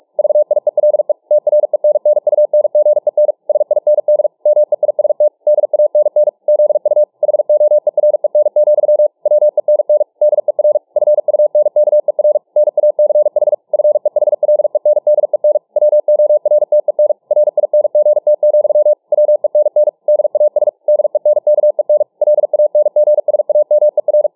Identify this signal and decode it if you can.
Morse code, heard as a keyed tone, about 44 wpm, 600 Hz. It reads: VIELE TRENNUNGEN SIND MEIST DANN ZU HOEREN, WENN DER FUNKER NACH FEHLENDEN WORTEN RINGT, WENN DAS DENKEN LANGSAMER